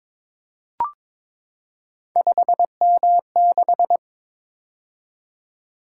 Morse code, heard as a keyed tone, about 22 wpm, 700 Hz. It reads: 5M6
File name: practice / 22wpm / Three Letter-Number Combinations - 22wpm